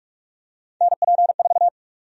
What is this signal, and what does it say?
Morse code, heard as a keyed tone, about 42 wpm, 700 Hz. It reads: NP4
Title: USA Call Sign Prefixes - 45wpm